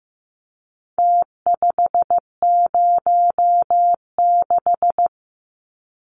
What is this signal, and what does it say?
Morse code, heard as a keyed tone, about 15 wpm, 700 Hz. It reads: T506
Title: Numbers - Contest Style - 4-Digit 15wpm